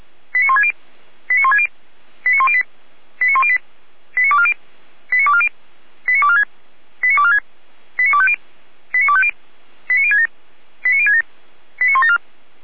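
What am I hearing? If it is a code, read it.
Five-tone selective call (ZVEI), heard as quick runs of five tones, tones of about 80 ms. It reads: 89160 89160 89108 89108 89240 89240 89246 89246 89250 89250 89067 89067 89174
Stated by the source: Five-tone ZVEI selective call 89336 received by SDR.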